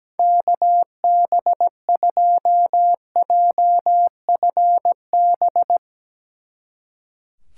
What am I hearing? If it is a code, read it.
Morse code, heard as a keyed tone, about 17 wpm, 700 Hz. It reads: KB2JFB
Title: Speed Racing - USA 2x3 Call Signs 17wpm